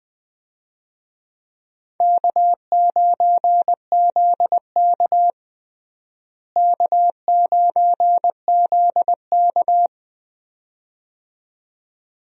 Morse code, heard as a keyed tone, about 20 wpm, 700 Hz. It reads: K9ZK K9ZK
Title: USA Call Signs - 4 Characters - Repeated 2x - 20wpm